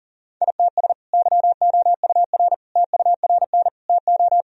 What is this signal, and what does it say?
Morse code, heard as a keyed tone, about 40 wpm, 700 Hz. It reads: ITS YOUR TURN TO
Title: Sentences from Top 1000 Words - Review Part 1 - 40wpm